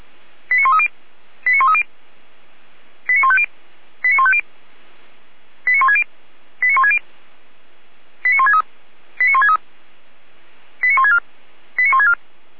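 Five-tone selective call (ZVEI), heard as quick runs of five tones, tones of about 80 ms. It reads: 89130 89130 89150 89150 89160 89160 89163 89163 89164 89164
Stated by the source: ZVEI five-tone selective call received on an SDR.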